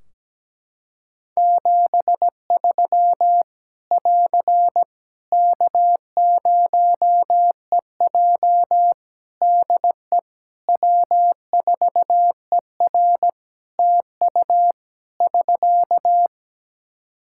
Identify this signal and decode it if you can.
Morse code, heard as a keyed tone, about 17 wpm, 700 Hz. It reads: 73 <AR> K0EJ DE W4ER TU <SK>
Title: Practice QSO - Exchange 4 - Ending - 17wpm